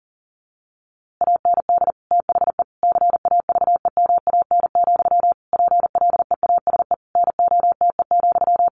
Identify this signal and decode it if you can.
Morse code, heard as a keyed tone, about 40 wpm, 700 Hz. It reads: AND THE CAVEMAN, PLEASE NOTE,
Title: A Sound Of Thunder 40wpm